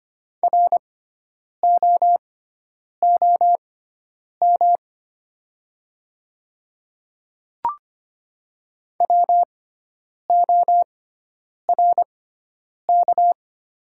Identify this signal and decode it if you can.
Morse code, heard as a keyed tone, about 25 wpm, spaced as for 9 wpm, 700 Hz. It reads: ROOM WORK
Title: 6x Character Spacing - Top 300 Words - 25wpm